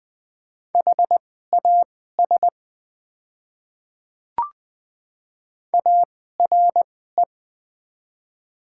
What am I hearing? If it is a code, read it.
Morse code, heard as a keyed tone, about 20 wpm, spaced as for 15 wpm, 700 Hz. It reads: HAS ARE
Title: 2x Character Spacing - Top 100 Words - 20wpm